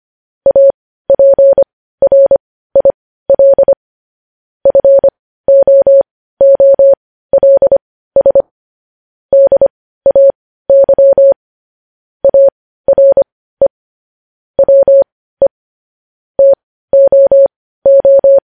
Morse code, heard as a keyed tone, about 25 wpm, spaced as for 15 wpm, 550 Hz. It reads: APRIL FOOLS DAY ARE WE TOO
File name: podcast / News headlines-1x-25x15